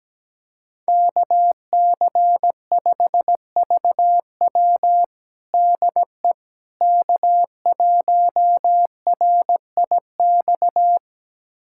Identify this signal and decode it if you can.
Morse code, heard as a keyed tone, about 17 wpm, 700 Hz. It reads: KC5VW DE K1RIX
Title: USA Callsign DE Callsign 17wpm